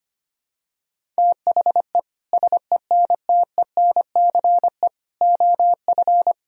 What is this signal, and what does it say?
Morse code, heard as a keyed tone, about 25 wpm, 700 Hz. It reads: THE SENTENCE OF
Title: Sentences from Top 1000 Words - Review Part 1 - 25wpm